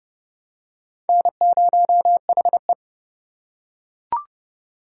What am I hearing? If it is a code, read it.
Morse code, heard as a keyed tone, about 30 wpm, 700 Hz. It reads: N0HE
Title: USA Call Signs - 4 Characters - 3x Character Spacing - 30wpm